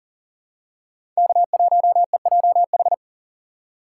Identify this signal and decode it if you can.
Morse code, heard as a keyed tone, about 40 wpm, 700 Hz. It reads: K1EJH
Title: USA Call Signs - 3x Character Spacing - 40wpm